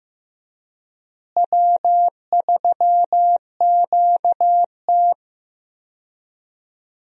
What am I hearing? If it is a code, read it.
Morse code, heard as a keyed tone, about 16 wpm, 700 Hz. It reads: W3QT
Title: USA Call Signs - 4 Characters - Repeated 2x - 15wpm